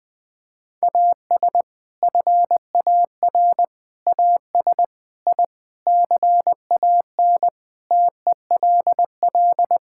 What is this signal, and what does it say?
Morse code, heard as a keyed tone, about 20 wpm, 700 Hz. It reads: AS FAR AS I CAN TELL
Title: Sentences with 6-Word Phrases - Spoken Beginning - 20wpm